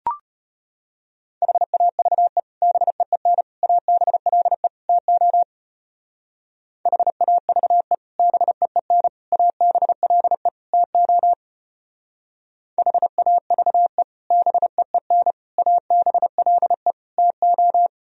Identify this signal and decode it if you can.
Morse code, heard as a keyed tone, about 33 wpm, 700 Hz. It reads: HAVE BEEN ABLE TO HAVE BEEN ABLE TO HAVE BEEN ABLE TO